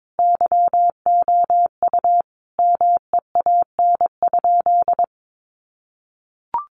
Morse code, heard as a keyed tone, about 22 wpm, 700 Hz. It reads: YOU MEAN?
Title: Sentences from Top 900 Words 22wpm